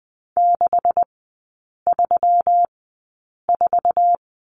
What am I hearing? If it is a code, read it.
Morse code, heard as a keyed tone, about 20 wpm, spaced as for 8 wpm, 700 Hz. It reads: 634
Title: 152 Lesson 25e - Letter p - Three Characters 20wpm